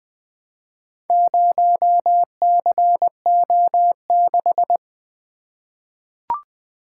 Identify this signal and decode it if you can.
Morse code, heard as a keyed tone, about 20 wpm, 700 Hz. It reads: 0CO6
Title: Four Letter-Number Combinations - 20wpm